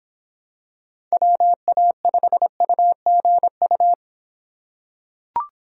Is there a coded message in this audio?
Yes: Morse code from 1.1 to 3.9 s, heard as a keyed tone, about 25 wpm, 700 Hz. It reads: WA5UGU